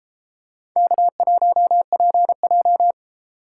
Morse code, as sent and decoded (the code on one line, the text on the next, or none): -.- .---- .--. .---
K1PJ